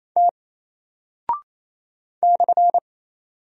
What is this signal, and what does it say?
Morse code, heard as a keyed tone, about 28 wpm, 700 Hz. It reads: T /
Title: Single Character (Letter:Number:Punctuation:Prosign) Alt 28wpm